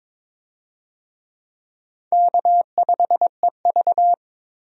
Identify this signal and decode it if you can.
Morse code, heard as a keyed tone, about 22 wpm, 700 Hz. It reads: K5EV
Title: USA Call Signs - 4 Characters - 22wpm